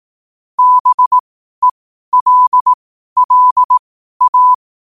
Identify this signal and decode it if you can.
Morse code, heard as a keyed tone, about 18 wpm, spaced as for 12 wpm, 1000 Hz. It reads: BELLA
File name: qotdfiles / 12wpm